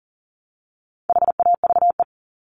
Morse code, heard as a keyed tone, about 40 wpm, 700 Hz. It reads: HAVE
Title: Top 300 Words 40wpm